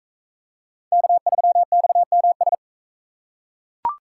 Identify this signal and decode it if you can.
Morse code, heard as a keyed tone, about 39 wpm, 700 Hz. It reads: K3XMS